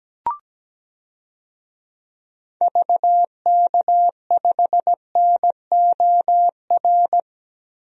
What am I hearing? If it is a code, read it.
Morse code, heard as a keyed tone, about 17 wpm, 700 Hz. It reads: VK5NOR
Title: DXCC-100 Call Signs - 6 Characters - Repeated 3x - 17wpm